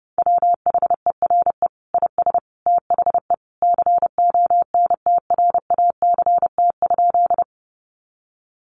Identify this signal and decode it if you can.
Morse code, heard as a keyed tone, about 30 wpm, 700 Hz. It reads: WHERE IS THE CONTRACT?
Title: Sentences from Top 1400 Words 30wpm